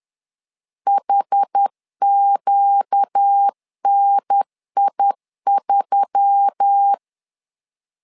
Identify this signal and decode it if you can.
Morse code, heard as a keyed tone, about 10 wpm, 800 Hz. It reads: HQNI3
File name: LEZIONI / 5 LBVC / 5 L 10wpm 2259